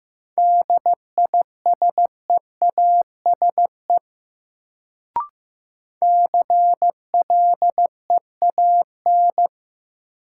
Morse code, heard as a keyed tone, about 15 wpm, 700 Hz. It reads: DISEASE CLEAN
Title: Sentences from Top 1100 Words - New Word Drill Alt - 15wpm